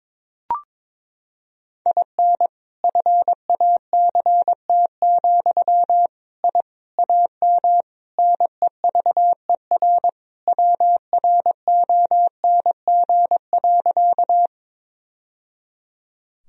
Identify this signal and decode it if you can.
Morse code, heard as a keyed tone, about 21 wpm, 700 Hz. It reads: IN FACT, I AM NEVER WRONG.
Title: Sentences from Top 1000 Words - Review Part 3 - 22wpm